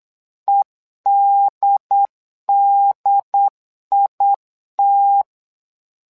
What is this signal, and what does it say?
Morse code, heard as a keyed tone, about 8 wpm, 800 Hz. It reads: EDDIT
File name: LEZIONI / 2 UHDN / 2 UHD 8wpm 2258